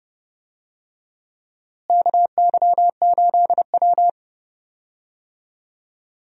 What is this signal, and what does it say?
Morse code, heard as a keyed tone, about 32 wpm, 700 Hz. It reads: KY8W
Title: USA Call Signs - 4 Characters - Repeated 3x - 30wpm